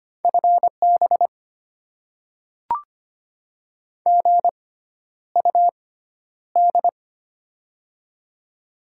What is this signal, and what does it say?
Morse code, heard as a keyed tone, about 25 wpm, spaced as for 11 wpm, 700 Hz. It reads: FB GUD
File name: Top 20 QSO Elements - 6x Character Spacing - 25wpm